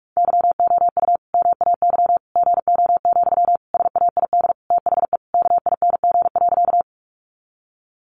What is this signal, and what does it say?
Morse code, heard as a keyed tone, about 44 wpm, 700 Hz. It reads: YOU MAY GO, SAID THE KING.
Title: Sentences from Top 400 Words 45wpm